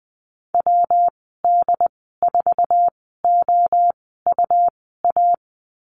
Morse code, heard as a keyed tone, about 20 wpm, spaced as for 14 wpm, 700 Hz. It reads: WD4OUA
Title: 207 Lesson 33g - Letter j - Call Signs 20wpm